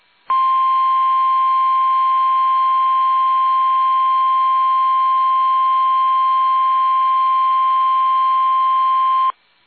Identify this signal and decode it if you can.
Emergency Alert System (SAME) tones (the 1050 Hz attention tone). It is heard from 0.2 to 9.4 s.
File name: eas / KUSO / KUSO